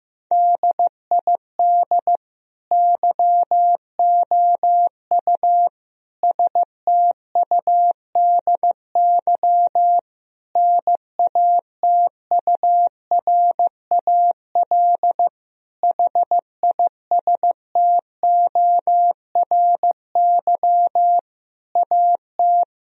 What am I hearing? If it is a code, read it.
Morse code, heard as a keyed tone, about 15 wpm, 700 Hz. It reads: DID YOU STUDY NATURAL HISTORY AT
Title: Sentences from Top 1000 Words - Review Part 4 - 15wpm